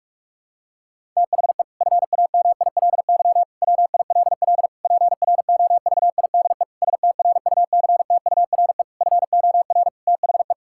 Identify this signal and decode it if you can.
Morse code, heard as a keyed tone, about 41 wpm, 700 Hz. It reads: THE FAMILY WILL PROVIDE STRUCTURE FOR THE
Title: Sentences from Top 1000 Words - Review Part 6 - 45wpm